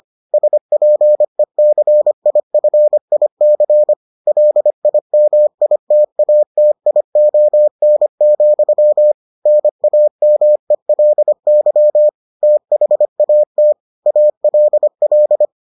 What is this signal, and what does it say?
Morse code, heard as a keyed tone, about 25 wpm, 600 Hz. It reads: SPECIFIC LIMITATION, NAMELY THAT ALL